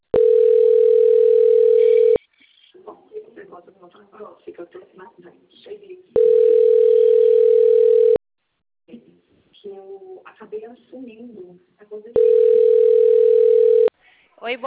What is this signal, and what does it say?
Ringback tone (North America), 440 + 480 Hz, with speech.
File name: rg-1600-unknown- - - .14552